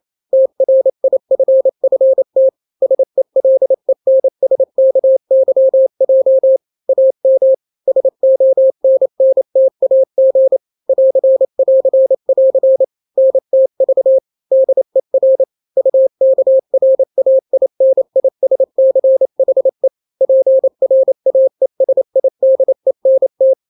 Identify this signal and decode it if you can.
Morse code, heard as a keyed tone, about 28 wpm, 530 Hz. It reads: TRIFFT SELENSKYJ AM SONNTAG <AR><AR><AR> NTV DER UKRAINISCHE PRAESIDENT
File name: daily / 27wpm